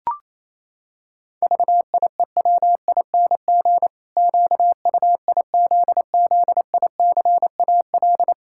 Morse code, heard as a keyed tone, about 28 wpm, 700 Hz. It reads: VIEWING QUIZZICAL
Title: Pangram Challenge v2 - 28wpm